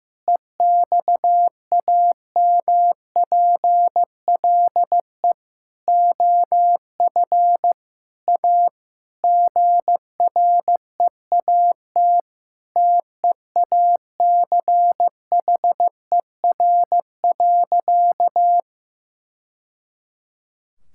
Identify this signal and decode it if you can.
Morse code, heard as a keyed tone, about 15 wpm, 700 Hz. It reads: EXAMPLE OF A GREAT TEACHER.
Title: Sentences from Top 1000 Words - Review Part 6 - 15wpm